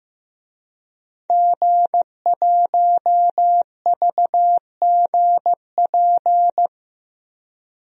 Morse code, heard as a keyed tone, about 15 wpm, 700 Hz. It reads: G1VGP
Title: DXCC-100 Call Signs - 5 Characters - 15wpm